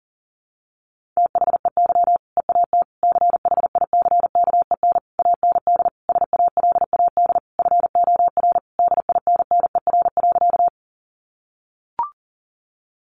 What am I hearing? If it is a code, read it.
Morse code, heard as a keyed tone, about 40 wpm, 700 Hz. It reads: THEY EAT CHICKEN AND SALAD FOR DINNER.
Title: Easy Phrases 40wpm